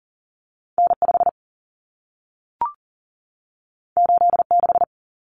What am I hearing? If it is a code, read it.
Morse code, heard as a keyed tone, about 40 wpm, 700 Hz. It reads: N5 86
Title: Common Cut Numbers - 2-Digit 40wpm